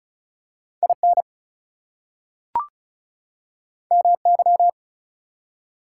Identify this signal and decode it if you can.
Morse code, heard as a keyed tone, about 34 wpm, 700 Hz. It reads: IN MY
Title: Two Letter Words - 35wpm